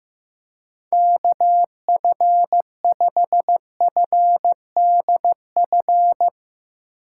Morse code, heard as a keyed tone, about 15 wpm, 700 Hz. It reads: KF5FDF
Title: USA Call Signs - 6 Characters - 2x Character Spacing - 15wpm